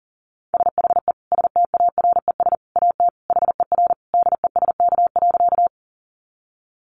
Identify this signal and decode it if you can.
Morse code, heard as a keyed tone, about 40 wpm, 700 Hz. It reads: SHE STARES AT HER DESK.
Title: Easy Phrases 40wpm